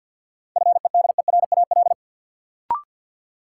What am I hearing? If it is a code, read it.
Morse code, heard as a keyed tone, about 52 wpm, 700 Hz. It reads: FEDERAL
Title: Sentences from Top 1000 Words - Review Part 5 - New Word Drill - 50wpm